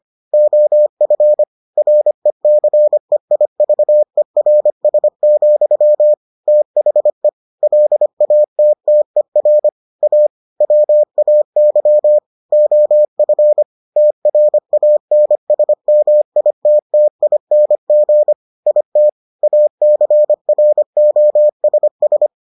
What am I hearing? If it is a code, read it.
Morse code, heard as a keyed tone, about 25 wpm, 600 Hz. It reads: OF RECEIVERS, THE LATTER A WAY OF TRANSMITTING IT ACROSS